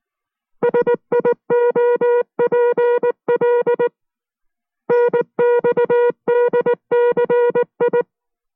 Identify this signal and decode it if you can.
Morse code, heard as a keyed tone, about 19 wpm, 480 Hz. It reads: SIOPL NXDCI